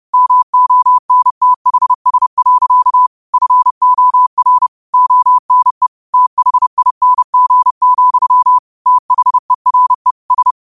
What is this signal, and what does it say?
Morse code, heard as a keyed tone, about 30 wpm, 1000 Hz. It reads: MONTHS. FOR ONE THING, THERE S